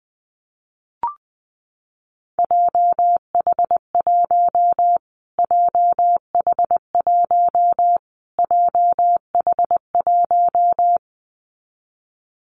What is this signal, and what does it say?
Morse code, heard as a keyed tone, about 20 wpm, 700 Hz. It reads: JH1 JH1 JH1
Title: Callsign Prefixes for Top 25 DXCC - Repeated 3x - 20wpm